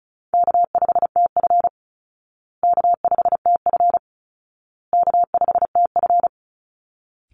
Morse code, heard as a keyed tone, about 34 wpm, 700 Hz. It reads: K5TF K5TF K5TF
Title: Call Signs Repeated 2x - 35wpm